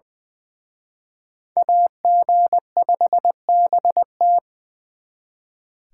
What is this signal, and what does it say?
Morse code, heard as a keyed tone, about 24 wpm, 700 Hz. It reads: AG5BT
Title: Speed Racing - USA Call Signs 20wpm